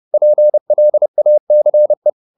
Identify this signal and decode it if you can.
Morse code, heard as a keyed tone, about 30 wpm, 600 Hz. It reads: PLACE